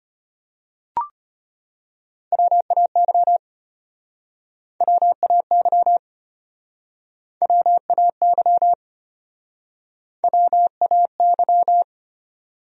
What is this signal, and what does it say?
Morse code, heard as a keyed tone, about 33 wpm, 700 Hz. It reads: WAY WAY WAY WAY